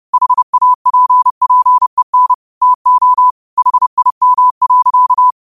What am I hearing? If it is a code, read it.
Morse code, heard as a keyed tone, about 30 wpm, 1000 Hz. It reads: HAPPEN TO HIM.